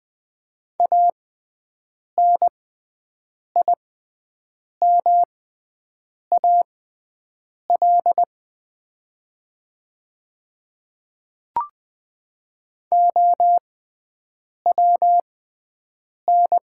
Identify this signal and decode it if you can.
Morse code, heard as a keyed tone, about 20 wpm, spaced as for 7 wpm, 700 Hz. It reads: ANIMAL OWN
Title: 6x Character Spacing - Top 300 Words - 20wpm